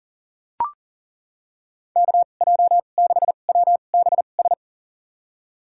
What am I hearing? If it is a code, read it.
Morse code, heard as a keyed tone, about 40 wpm, spaced as for 29 wpm, 700 Hz. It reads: KJ6WBS
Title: USA Call Signs - 6 Characters - 2x Character Spacing - 40wpm